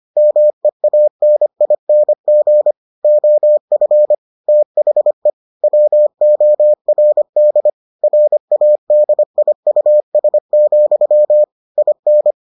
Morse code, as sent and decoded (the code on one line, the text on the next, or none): -- . .- -. .. -. --. / --- ..-. / - .... / .-- --- .-. -.. / .-. .- -.. .. ..- ... --..-- / .. -.
MEANING OF THE WORD RADIUS, IN